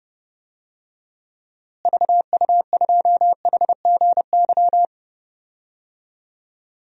Morse code, heard as a keyed tone, about 30 wpm, 700 Hz. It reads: VU2HGY
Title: DXCC-100 Call Signs - 6 Characters - Repeated 3x - 30wpm